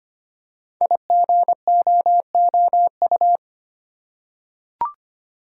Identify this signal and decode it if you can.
Morse code, heard as a keyed tone, about 25 wpm, 700 Hz. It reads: IGOOU
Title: Code Groups - 5 Letters - 25wpm